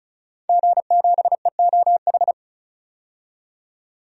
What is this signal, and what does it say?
Morse code, heard as a keyed tone, about 36 wpm, 700 Hz. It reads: G7EOH
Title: DXCC-100 Call Signs - 5 Characters - 35wpm